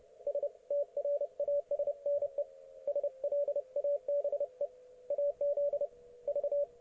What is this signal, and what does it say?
Morse code, heard as a keyed tone, about 30 wpm, 570 Hz, with radio noise. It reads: S TRASNE SLABE AZ V
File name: ok1fhi